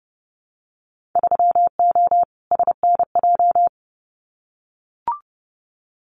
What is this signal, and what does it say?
Morse code, heard as a keyed tone, about 30 wpm, 700 Hz. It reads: 3O SNJ
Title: Winter Field Day Exchanges Encore 30wpm